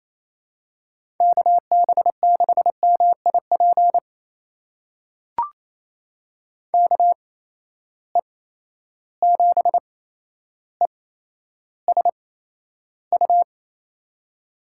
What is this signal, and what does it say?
Morse code, heard as a keyed tone, about 28 wpm, spaced as for 12 wpm, 700 Hz. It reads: KB6MIP KE7ESU
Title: USA Call Signs - 6 Characters - 8x Character Spacing - 28wpm